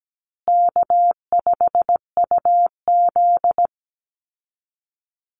Morse code, heard as a keyed tone, about 17 wpm, 700 Hz. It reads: K5UZ
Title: Call Signs Repeated 2x - 17wpm